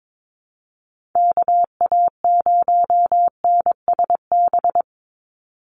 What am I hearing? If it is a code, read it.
Morse code, heard as a keyed tone, about 22 wpm, 700 Hz. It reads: KA0NSB
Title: USA Call Signs 22wpm